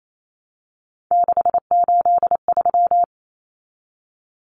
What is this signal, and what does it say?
Morse code, heard as a keyed tone, about 28 wpm, 700 Hz. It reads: 683
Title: Common Cut Numbers - 3-Digit 28wpm